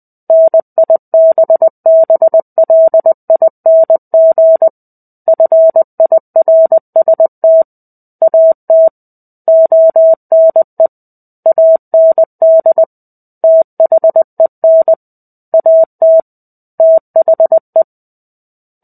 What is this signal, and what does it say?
Morse code, heard as a keyed tone, about 20 wpm, 650 Hz. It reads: NIBBLING FIRST AT ONE AND THEN AT THE
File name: 05-Chapter-05 20 WPM